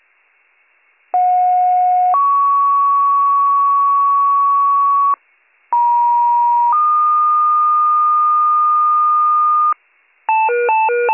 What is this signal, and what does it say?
Two-tone page, heard as a one-second tone, then a three-second tone, about 730 then 1120 Hz, about 950 then 1220 Hz. It is heard from 1.1 to 9.7 s.